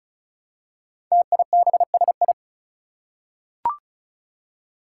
Morse code, heard as a keyed tone, about 36 wpm, 700 Hz. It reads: TIBSI
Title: Five Challenging Letter-Number Combinations 35wpm